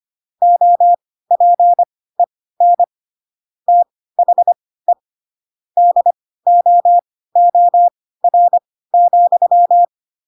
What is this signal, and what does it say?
Morse code, heard as a keyed tone, about 25 wpm, spaced as for 16 wpm, 700 Hz. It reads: OPEN THE DOOR,